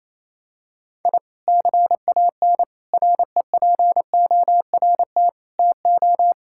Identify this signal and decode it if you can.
Morse code, heard as a keyed tone, about 28 wpm, 700 Hz. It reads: I CAN REPORT TO
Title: Sentences from Top 1000 Words - Review Part 6 - 28wpm